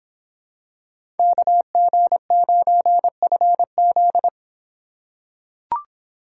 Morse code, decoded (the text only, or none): KG9FZ